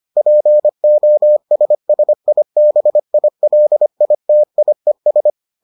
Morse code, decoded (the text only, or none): POSSIBILITIES